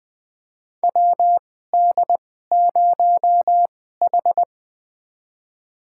Morse code, heard as a keyed tone, about 20 wpm, spaced as for 14 wpm, 700 Hz. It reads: WD0H
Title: USA Call Signs - 4 Characters - 2x Character Spacing - 20wpm